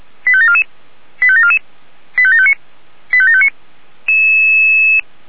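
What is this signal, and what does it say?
Five-tone selective call (ZVEI), heard as quick runs of five tones, tones of about 70 ms. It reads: 85744 85744 85759 85759
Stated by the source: Five-tone ZVEI selective call received on SDR.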